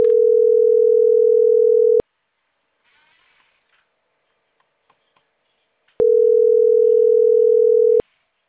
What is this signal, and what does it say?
Ringback tone (North America), 440 + 480 Hz.